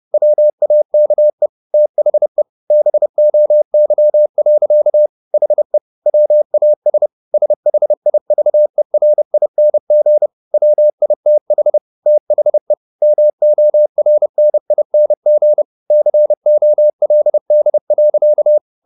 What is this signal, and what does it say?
Morse code, heard as a keyed tone, about 30 wpm, 600 Hz. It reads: WAKE THE BOY. HE WAS SHIVERING WITH THE MORNING COLD.
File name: the old man and the sea / the old man and the sea 1 30wpm